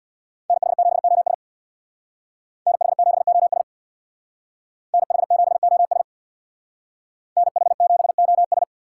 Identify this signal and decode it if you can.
Morse code, heard as a keyed tone, about 60 wpm, 700 Hz. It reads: NH7OS NH7OS NH7OS NH7OS
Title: Speed Racing - USA 2x2 Call Signs 50wpm